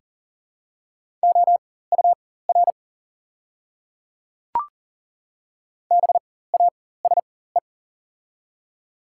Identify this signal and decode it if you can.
Morse code, heard as a keyed tone, about 38 wpm, spaced as for 18 wpm, 690 Hz. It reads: OUR BASE
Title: 4x Character Spacing - Top 500 Words - 40wpm